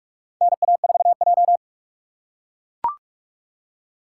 Morse code, heard as a keyed tone, about 44 wpm, 700 Hz. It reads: NA4J
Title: USA Call Signs - 4 Characters - 4x Character Spacing - 45wpm